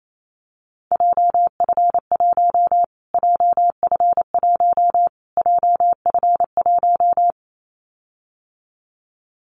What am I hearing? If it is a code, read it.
Morse code, heard as a keyed tone, about 28 wpm, 700 Hz. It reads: JF1 JF1 JF1
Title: Callsign Prefixes for Top 25 DXCC - Repeated 3x - 28wpm